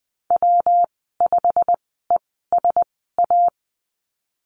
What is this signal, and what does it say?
Morse code, heard as a keyed tone, about 20 wpm, spaced as for 15 wpm, 700 Hz. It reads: W5ESA
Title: 207 Lesson 33g - Letter j - Call Signs 20wpm